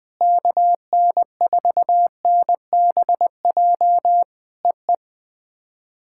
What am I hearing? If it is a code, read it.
Morse code, heard as a keyed tone, about 20 wpm, 700 Hz. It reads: KN4NBJ EE
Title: POTA QSO Practice - 20wpm